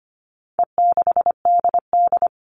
Morse code, heard as a keyed tone, about 25 wpm, 700 Hz. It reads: E6DD
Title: All Cut Numbers - 4-Digit 25wpm